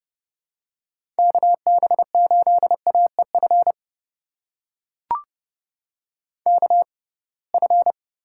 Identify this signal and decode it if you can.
Morse code, heard as a keyed tone, about 30 wpm, spaced as for 23 wpm, 700 Hz. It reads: KB8AEF KF
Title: USA Call Signs - 6 Characters - 6x Character Spacing - 30wpm